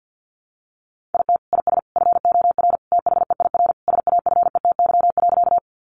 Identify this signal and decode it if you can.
Morse code, heard as a keyed tone, about 49 wpm, 700 Hz. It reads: IT IS FOR THEIR SAFETY.
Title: Sentences from Top 700 Words 50wpm